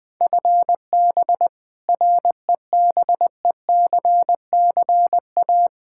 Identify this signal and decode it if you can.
Morse code, heard as a keyed tone, about 20 wpm, 700 Hz. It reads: FB REBECCA
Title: Practice QSO - Exchange 3 - More Info - 20wpm